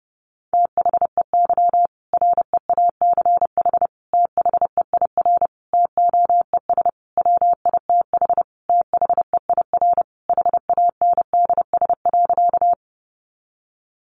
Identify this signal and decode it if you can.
Morse code, heard as a keyed tone, about 30 wpm, 700 Hz. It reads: THEY REACH THEIR TOES WITH THEIR HANDS.
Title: Easy Phrases 30wpm